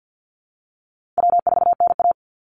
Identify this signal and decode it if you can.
Morse code, heard as a keyed tone, about 50 wpm, 700 Hz. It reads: W4NA
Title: Call Signs Repeated 2x - 50wpm